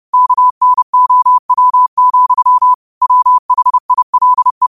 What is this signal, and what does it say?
Morse code, heard as a keyed tone, about 30 wpm, 1000 Hz. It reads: KNOW, WHILE